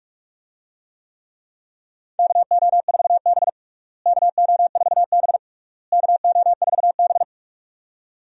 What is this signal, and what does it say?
Morse code, heard as a keyed tone, about 46 wpm, 700 Hz. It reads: KO4B KO4B KO4B
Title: USA Call Signs - 4 Characters - Repeated 3x - 45wpm